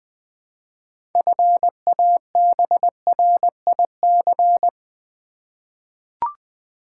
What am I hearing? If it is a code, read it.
Morse code, heard as a keyed tone, about 20 wpm, 700 Hz. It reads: FABRIC
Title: 6 Letter Words - First 3 Letters Given - 20wpm